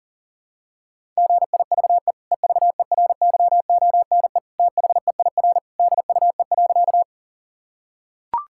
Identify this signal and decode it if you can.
Morse code, heard as a keyed tone, about 40 wpm, 700 Hz. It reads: GIVE EVERYONE THEIR DUE.
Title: Sentences from Top 1000 Words - Review Part 6 - 40wpm